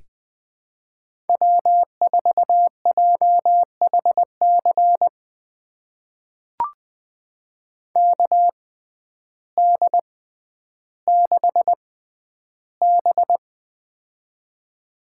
Morse code, heard as a keyed tone, about 20 wpm, spaced as for 11 wpm, 700 Hz. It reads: W4JHC KD6B
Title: USA Call Signs - 6x Character Spacing - 20wpm